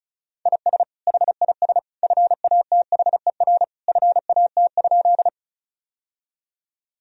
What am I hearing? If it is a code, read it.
Morse code, heard as a keyed tone, about 36 wpm, 700 Hz. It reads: IS HIS FATHER FAT?
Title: Sentences from Top 1000 Words - Review Part 4 - 35wpm